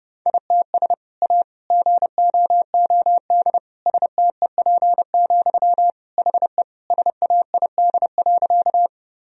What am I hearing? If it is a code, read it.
Morse code, heard as a keyed tone, about 32 wpm, 700 Hz. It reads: ITS A GOOD STEP, HE SAID.
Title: Sentences from Top 1000 Words - Review Part 2 - 30wpm